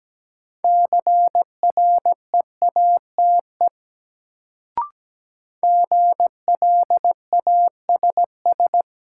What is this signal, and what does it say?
Morse code, heard as a keyed tone, about 17 wpm, 700 Hz. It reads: CREATE GLASS
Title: Sentences from Top 1000 Words - Review Part 3 - New Word Drill - 17wpm